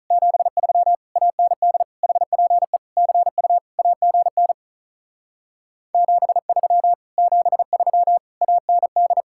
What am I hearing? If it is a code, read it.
Morse code, heard as a keyed tone, about 44 wpm, 700 Hz. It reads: 73 AND HPE CU AGN 73 73 AND